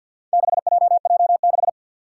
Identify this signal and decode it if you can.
Morse code, heard as a keyed tone, about 51 wpm, 700 Hz. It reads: 6JJ6
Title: Four Challenging Letter-Number Combinations 50wpm